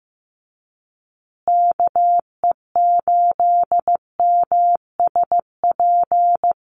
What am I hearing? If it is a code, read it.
Morse code, heard as a keyed tone, about 15 wpm, 700 Hz. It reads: KE8MSP
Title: USA Call Signs Repeated 2x 15wpm